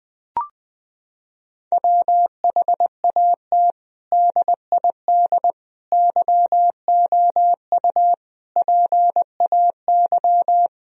Morse code, heard as a keyed tone, about 19 wpm, 700 Hz. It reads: WHAT DID YOU PAY
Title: Sentences from Top 1000 Words - Review Part 3 - 20wpm